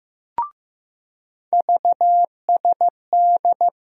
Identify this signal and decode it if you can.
Morse code, heard as a keyed tone, about 15 wpm, 700 Hz. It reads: VSD
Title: USA Call Sign Postfixes - 3 Letters 15wpm